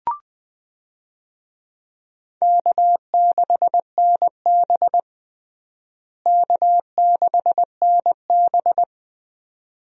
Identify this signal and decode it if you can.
Morse code, heard as a keyed tone, about 21 wpm, 700 Hz. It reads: K6NB K6NB
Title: USA Call Signs - 4 Characters - Repeated 2x - 20wpm